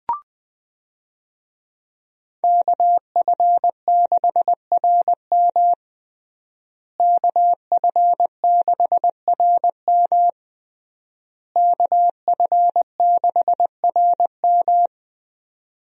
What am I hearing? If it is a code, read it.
Morse code, heard as a keyed tone, about 20 wpm, 700 Hz. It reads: KF6RM KF6RM KF6RM
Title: USA Call Signs - 5 Characters - Repeated 3x - 20wpm